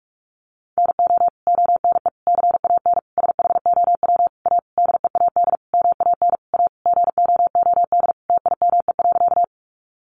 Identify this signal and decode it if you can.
Morse code, heard as a keyed tone, about 44 wpm, 700 Hz. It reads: NO ONE CAN SHOW A DEAD MAN A GOOD TIME.
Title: Sentences from Top 1500 Words 45wpm